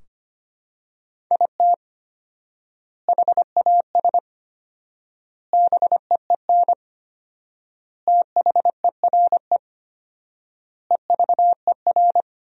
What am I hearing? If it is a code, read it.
Morse code, heard as a keyed tone, about 25 wpm, 700 Hz. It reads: IT HAS BEEN THERE EVER
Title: 4x Word-Spacing - Sentences from Top 400 Words - 25wpm